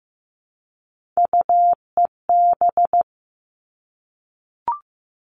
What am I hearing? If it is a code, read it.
Morse code, heard as a keyed tone, about 15 wpm, 700 Hz. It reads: UEB
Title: All Cut Numbers - 3-Digit 15wpm